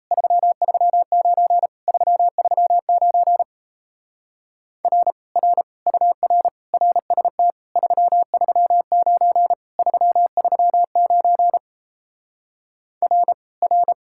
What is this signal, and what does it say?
Morse code, heard as a keyed tone, about 37 wpm, 700 Hz. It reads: 339 339 R R UR RST 339 339 R R